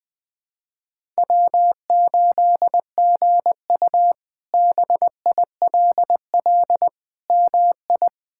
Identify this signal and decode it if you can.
Morse code, heard as a keyed tone, about 20 wpm, 700 Hz. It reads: W8GU BILL MI
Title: K1USN SST QSO Practice - 20wpm